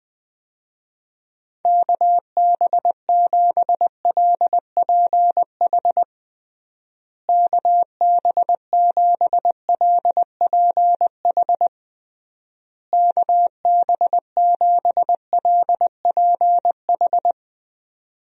Morse code, heard as a keyed tone, about 20 wpm, 700 Hz. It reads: KB7LPH KB7LPH KB7LPH
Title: USA Call Signs - 6 Characters - Repeated 3x - 20wpm